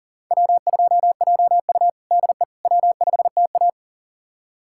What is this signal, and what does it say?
Morse code, heard as a keyed tone, about 40 wpm, 700 Hz. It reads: W2JU DE W5TA
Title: USA Callsign DE Callsign - 4 Characters - 40wpm